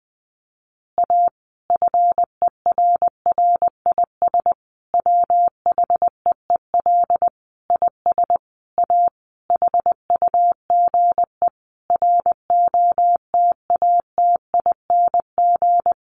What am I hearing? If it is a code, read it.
Morse code, heard as a keyed tone, about 20 wpm, 700 Hz. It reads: A FERRIS WHEEL IS A HUGE ROTATING
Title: Additional Easy Phrases 20wpm